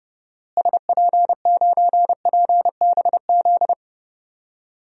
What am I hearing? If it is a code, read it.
Morse code, heard as a keyed tone, about 30 wpm, 700 Hz. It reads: SP9PBZ
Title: DXCC-100 Call Signs - 6 Characters - Repeated 2x - 30wpm